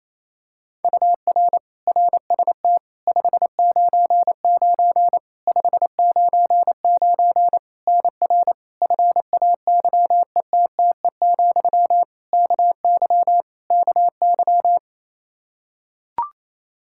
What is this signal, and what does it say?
Morse code, heard as a keyed tone, about 28 wpm, 700 Hz. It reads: UR RST 599 599 NR FAYETTE, KY KY
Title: Practice QSO - Exchange 1 - Beginning - 28wpm